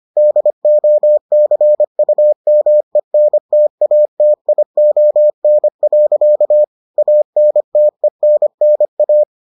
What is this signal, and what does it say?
Morse code, heard as a keyed tone, about 25 wpm, 600 Hz. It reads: DOCUMENTATION. ANTENNA